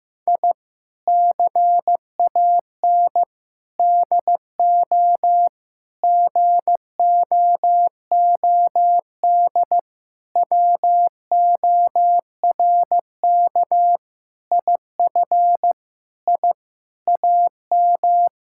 Morse code, heard as a keyed tone, about 16 wpm, 700 Hz. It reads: I CAN DO GOOD WORK IF I AM
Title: Sentences from Top 1000 Words - Review Part 6 - 15wpm